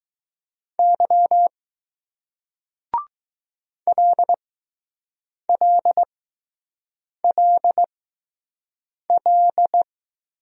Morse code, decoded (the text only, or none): Y LLLL